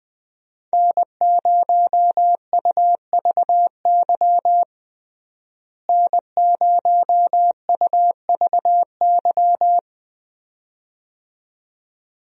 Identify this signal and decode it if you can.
Morse code, heard as a keyed tone, about 20 wpm, 700 Hz. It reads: N0UVY N0UVY
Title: USA Call Signs - 5 Characters - Repeated 2x - 20wpm